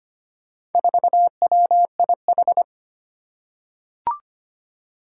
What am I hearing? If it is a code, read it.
Morse code, heard as a keyed tone, about 25 wpm, 700 Hz. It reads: 4WIH
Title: Four Challenging Letter-Number Combinations 25wpm